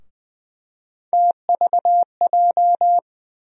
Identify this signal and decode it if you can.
Morse code, heard as a keyed tone, about 20 wpm, 700 Hz. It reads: TVJ
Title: USA Call Sign Postfixes - 3 Letters 20wpm